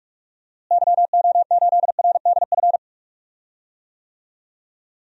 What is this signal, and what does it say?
Morse code, heard as a keyed tone, about 43 wpm, 700 Hz. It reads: YO8RDF
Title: DXCC-100 Call Signs - 6 Characters - Repeated 2x - 45wpm